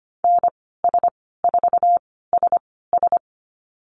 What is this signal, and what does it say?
Morse code, heard as a keyed tone, about 25 wpm, spaced as for 16 wpm, 700 Hz. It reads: NS4SS
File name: Lesson 15g - Number 5 - Call Signs 25wpm